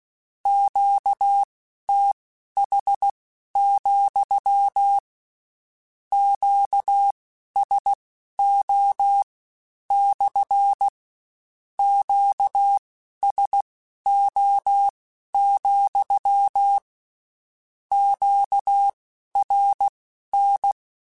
Morse code, heard as a keyed tone, about 16 wpm, spaced as for 11 wpm, 800 Hz. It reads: QTH, QSO / QSO, QRN